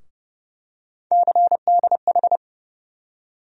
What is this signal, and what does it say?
Morse code, heard as a keyed tone, about 30 wpm, 700 Hz. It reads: CDH
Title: USA Call Sign Postfixes - 3 Letters 30wpm